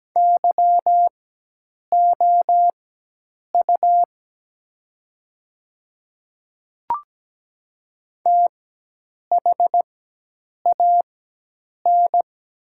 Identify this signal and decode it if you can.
Morse code, heard as a keyed tone, about 17 wpm, spaced as for 8 wpm, 700 Hz. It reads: YOU THAN
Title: 4x Character Spacing - Top 100 Words - 17wpm